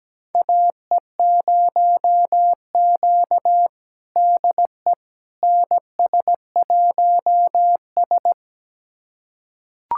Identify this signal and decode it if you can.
Morse code, heard as a keyed tone, about 17 wpm, 700 Hz. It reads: AE0Q DE NS1S
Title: USA Callsign DE Callsign - 4 Characters - 17wpm